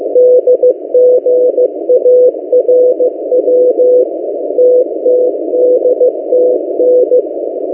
Morse code, heard as a keyed tone, about 15 wpm, 500 Hz. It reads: DGARW TTDTN
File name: M08a- -1355utc-8096khz